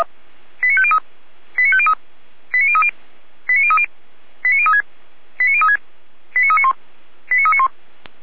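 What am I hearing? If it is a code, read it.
Five-tone selective call (ZVEI), heard as quick runs of five tones, tones of about 70 ms. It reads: 89503 89503 89030 89030 89036 89036 89391 89391
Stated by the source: ZVEI five-tone selective call received by SDR.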